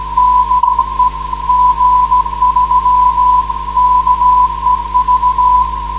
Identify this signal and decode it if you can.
Morse code, heard as a keyed tone, about 16 wpm, 1000 Hz. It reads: DE G3KEV
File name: 472khz web swl / 05januar2013 / g3kev 472.6khz